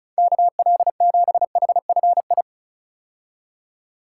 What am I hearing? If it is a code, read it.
Morse code, heard as a keyed tone, about 36 wpm, 700 Hz. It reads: KL7HFI
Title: K1USN SST QSO Practice - 35wpm